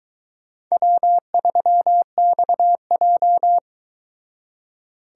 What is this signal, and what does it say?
Morse code, heard as a keyed tone, about 22 wpm, 700 Hz. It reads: W3XJ